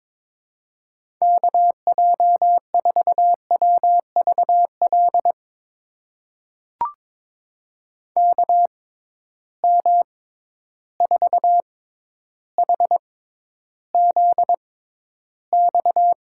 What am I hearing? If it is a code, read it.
Morse code, heard as a keyed tone, about 22 wpm, spaced as for 11 wpm, 700 Hz. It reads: KJ4WVL KM4HZX
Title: USA Call Signs - 6 Characters - 6x Character Spacing - 22wpm